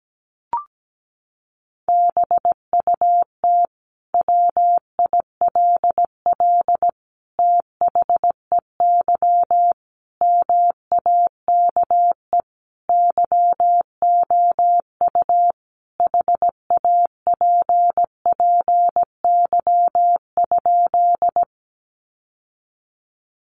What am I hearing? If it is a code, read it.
Morse code, heard as a keyed tone, about 17 wpm, 700 Hz. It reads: BUT WILL THEY MAKE YOU HAPPY?
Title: Sentences from Top 1000 Words 17wpm